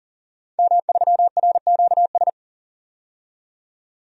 Morse code, heard as a keyed tone, about 40 wpm, 700 Hz. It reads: M3RQS
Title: DXCC-100 Call Signs - 5 Characters - 40wpm